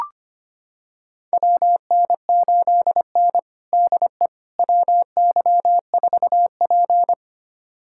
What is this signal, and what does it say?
Morse code, heard as a keyed tone, about 25 wpm, 700 Hz. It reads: WN8N DE WY4P
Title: USA Callsign DE Callsign - 4 Characters - 25wpm